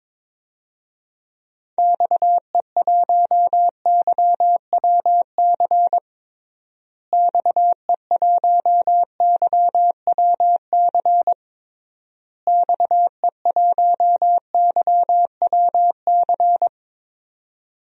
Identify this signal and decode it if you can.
Morse code, heard as a keyed tone, about 22 wpm, 700 Hz. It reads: XE1YWC XE1YWC XE1YWC
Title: DXCC-100 Call Signs - 6 Characters - Repeated 3x - 22wpm